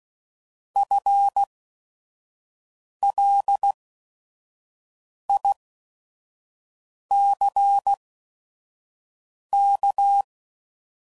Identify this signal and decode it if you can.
Morse code, heard as a keyed tone, about 16 wpm, spaced as for 5 wpm, 800 Hz. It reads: FLICK
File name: Prefixes / By First Letter / Fs